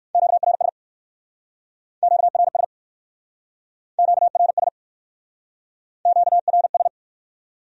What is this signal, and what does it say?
Morse code, heard as a keyed tone, about 57 wpm, 700 Hz. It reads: QRS QRS QRS QRS